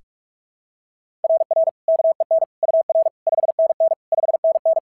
Morse code, heard as a keyed tone, about 46 wpm, 690 Hz. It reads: RR KEN UR 5NN 5NN
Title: SOTA QSO Practice 45wpm